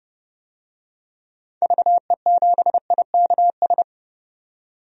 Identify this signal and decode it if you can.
Morse code, heard as a keyed tone, about 30 wpm, 700 Hz. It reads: VE7IKS
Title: DXCC-100 Call Signs - 6 Characters - 30wpm